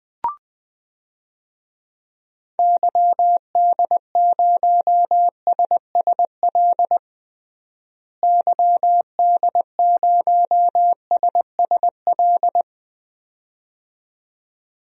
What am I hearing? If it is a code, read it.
Morse code, heard as a keyed tone, about 21 wpm, 700 Hz. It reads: YD0SSL YD0SSL
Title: DXCC-100 Call Signs - 6 Characters - Repeated 2x - 20wpm